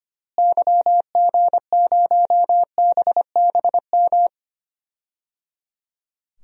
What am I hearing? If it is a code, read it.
Morse code, heard as a keyed tone, about 25 wpm, 700 Hz. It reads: YG0BBM
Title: DXCC-100 Call Signs - 6 Characters - Repeated 3x - 25wpm